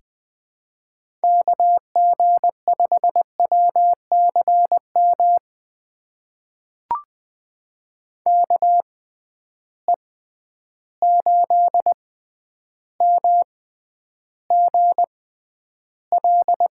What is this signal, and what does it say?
Morse code, heard as a keyed tone, about 20 wpm, spaced as for 10 wpm, 700 Hz. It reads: KG5WCM KE8MGL
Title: USA Call Signs - 6 Characters - 6x Character Spacing - 20wpm